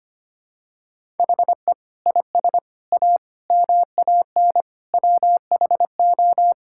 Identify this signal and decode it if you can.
Morse code, heard as a keyed tone, about 25 wpm, 700 Hz. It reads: HE IS A MAN WHO
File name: Sentences with 3-Word Phrases - 25wpm